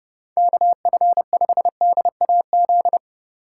Morse code, heard as a keyed tone, about 29 wpm, 700 Hz. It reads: KF5DAZ